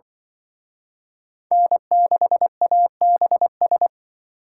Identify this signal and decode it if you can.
Morse code, heard as a keyed tone, about 25 wpm, 700 Hz. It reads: N6ABS